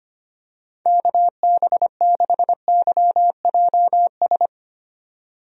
Morse code, heard as a keyed tone, about 25 wpm, 700 Hz. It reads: KB6YJS
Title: USA Call Signs - 6 Characters - 25wpm